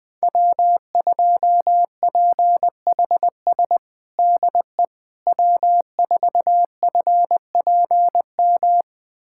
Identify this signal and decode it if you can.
Morse code, heard as a keyed tone, about 20 wpm, 700 Hz. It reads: W2PHS DE W4FPM
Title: USA Callsign DE Callsign - 5 Characters - 20wpm